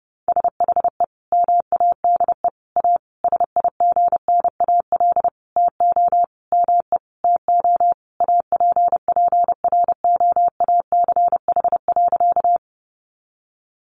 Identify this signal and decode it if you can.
Morse code, heard as a keyed tone, about 28 wpm, 700 Hz. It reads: SHE MADE A SIGNAL TO ME TO APPROACH.
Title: Sentences from Top 1400 Words 30wpm